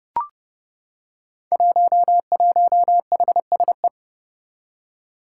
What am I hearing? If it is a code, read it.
Morse code, heard as a keyed tone, about 30 wpm, 700 Hz. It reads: 11HSE
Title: Five Challenging Letter-Number Combinations 30wpm